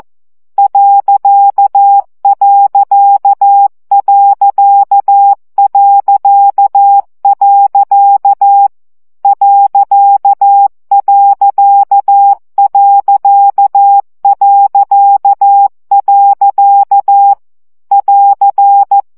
Morse code, heard as a keyed tone, about 14 wpm, 800 Hz. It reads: ..... ..... <AR>
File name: CW gyakorlas pont 12 baud